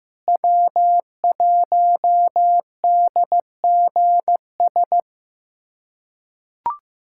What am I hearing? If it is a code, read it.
Morse code, heard as a keyed tone, about 15 wpm, 700 Hz. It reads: W1DGS
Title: USA Callsign DE Callsign - 5 Characters - 15wpm